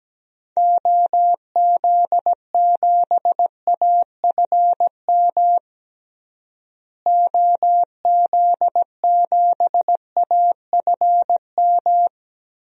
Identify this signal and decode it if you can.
Morse code, heard as a keyed tone, about 17 wpm, 700 Hz. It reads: OZ7AFM OZ7AFM
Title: DXCC-100 Call Signs - 6 Characters - Repeated 3x - 17wpm